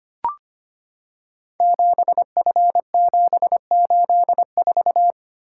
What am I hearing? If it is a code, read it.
Morse code, heard as a keyed tone, about 25 wpm, 700 Hz. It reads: 7F784
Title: Code Groups - 5 Letter:Numbers - 25wpm